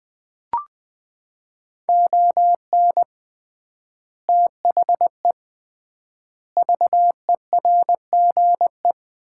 Morse code, heard as a keyed tone, about 20 wpm, 700 Hz. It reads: ON THE VERGE
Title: 3x Word-Spacing - Sets of 4 Words - 20wpm